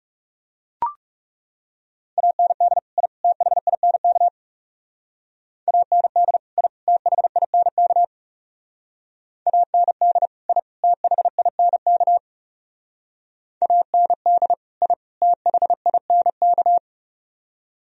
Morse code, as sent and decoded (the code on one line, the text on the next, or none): .- -. -.. / .. / - .... .. -. -.- / .- -. -.. / .. / - .... .. -. -.- / .- -. -.. / .. / - .... .. -. -.- / .- -. -.. / .. / - .... .. -. -.-
AND I THINK AND I THINK AND I THINK AND I THINK